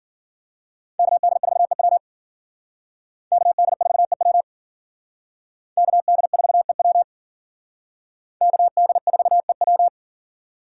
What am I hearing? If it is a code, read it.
Morse code, heard as a keyed tone, about 51 wpm, 700 Hz. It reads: KD4EW KD4EW KD4EW KD4EW